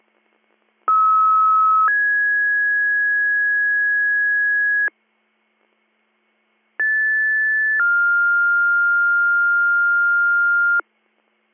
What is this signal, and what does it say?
Two-tone page, heard as a one-second tone, then a three-second tone, about 1290 then 1650 Hz, about 1750 then 1400 Hz. It is heard from 0.9 to 10.8 s.